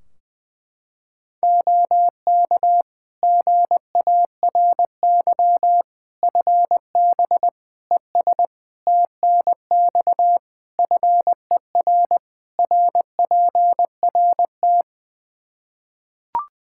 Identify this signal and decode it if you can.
Morse code, heard as a keyed tone, about 21 wpm, 700 Hz. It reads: OK GARY FB ES TNX FER RPRT
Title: Practice QSO - Exchange 2 - Info - 20wpm